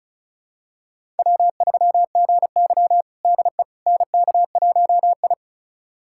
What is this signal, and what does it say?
Morse code, heard as a keyed tone, about 36 wpm, 700 Hz. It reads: W3GY DE NK1I
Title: USA Callsign DE Callsign - 4 Characters - 35wpm